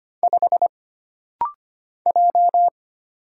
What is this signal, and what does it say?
Morse code, heard as a keyed tone, about 25 wpm, 700 Hz. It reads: 5 J
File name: Single Character (Letter:Number:Punctuation:Prosign) - Rapid-Fire - 25wpm